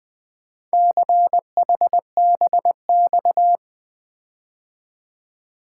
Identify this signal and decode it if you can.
Morse code, heard as a keyed tone, about 24 wpm, 700 Hz. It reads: CHBX
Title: Speed-Racing - Four Letter-Number Combinations - 20wpm